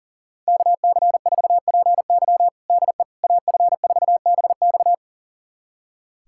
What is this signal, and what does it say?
Morse code, heard as a keyed tone, about 37 wpm, 700 Hz. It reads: KC4PY DE AF4BX
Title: USA Callsign DE Callsign - 5 Characters - 40wpm